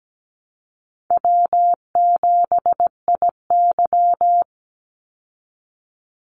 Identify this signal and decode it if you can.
Morse code, heard as a keyed tone, about 17 wpm, 700 Hz. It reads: W7IY
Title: Easy Call Signs Repeated 2x - 17wpm